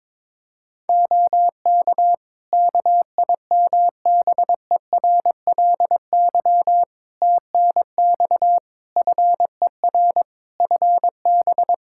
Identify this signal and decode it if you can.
Morse code, heard as a keyed tone, about 22 wpm, 700 Hz. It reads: OK KIMBERLY TNX FER FB
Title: Practice QSO - Exchange 4 - Ending - 22wpm